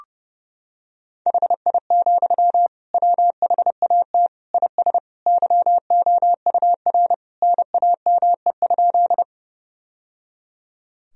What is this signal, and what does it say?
Morse code, heard as a keyed tone, about 30 wpm, 700 Hz. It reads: HI, WHAT IS YOUR NAME?
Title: Sentences from Top 1000 Words - Review Part 3 - 30wpm